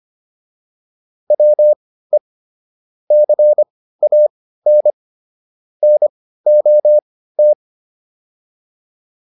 Morse code, heard as a keyed tone, about 25 wpm, spaced as for 15 wpm, 600 Hz. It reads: WE CAN NOT
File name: WD205 15